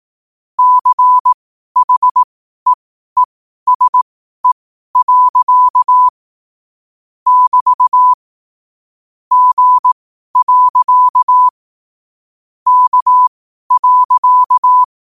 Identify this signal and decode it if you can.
Morse code, heard as a keyed tone, about 18 wpm, spaced as for 12 wpm, 1000 Hz. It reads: CHEESE. <BT> G. K.
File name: qotdfiles / 12wpm